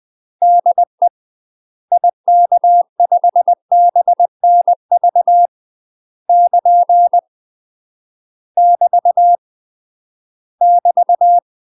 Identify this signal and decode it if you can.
Morse code, heard as a keyed tone, about 20 wpm, 700 Hz. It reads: DE IK5BNV <KN> <BT> <BT>
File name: UK MASTER QSOs TRAINING / 20WPM / UK MASTER QSOs 0044